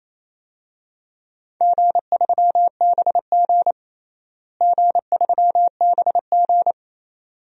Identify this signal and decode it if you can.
Morse code, heard as a keyed tone, about 27 wpm, 700 Hz. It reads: G3BG G3BG
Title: DXCC-100 Call Signs - 4 Characters - Repeated 2x - 28wpm